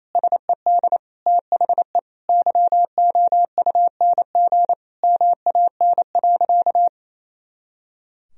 Morse code, heard as a keyed tone, about 28 wpm, 700 Hz. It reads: SED THE YOUNG MAN.
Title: Sentences from Top 400 Words and Abbreviations 28wpm